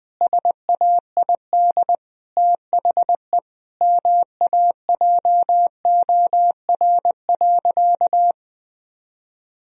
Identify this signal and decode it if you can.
Morse code, heard as a keyed tone, about 20 wpm, 700 Hz. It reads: SAID THE MAJOR.
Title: Sentences from Top 1000 Words - Review Part 4 - 20wpm